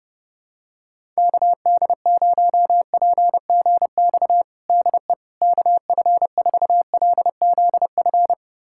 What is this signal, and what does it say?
Morse code, heard as a keyed tone, about 30 wpm, 700 Hz. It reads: KD0PGX DE KF4LZF
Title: USA Callsign DE Callsign 30wpm